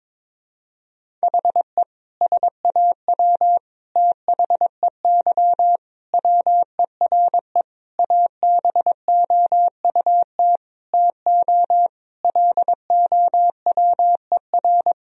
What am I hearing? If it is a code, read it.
Morse code, heard as a keyed tone, about 22 wpm, 700 Hz. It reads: HE SAW THEY WERE ABOUT TO LOWER
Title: Sentences from Top 1000 Words - Review Part 4 - 22wpm